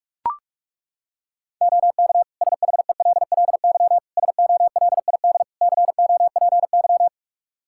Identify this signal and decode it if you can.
Morse code, heard as a keyed tone, about 42 wpm, 700 Hz. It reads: OK SHELLY SOLID COPY
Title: Practice QSO - Exchange 3 - More Info - 45wpm